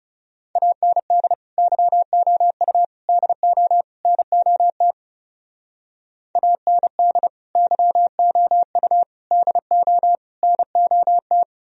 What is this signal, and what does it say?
Morse code, heard as a keyed tone, about 36 wpm, 700 Hz. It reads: AND YOU DO NOT AND YOU DO NOT
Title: Speed-Racing - Sets of 4 Words 30wpm